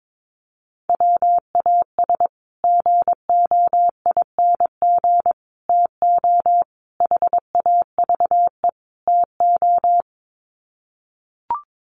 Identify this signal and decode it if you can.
Morse code, heard as a keyed tone, about 22 wpm, 700 Hz. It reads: WAS GOING TO HAVE TO
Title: Sets of 5 Words Encore 22wpm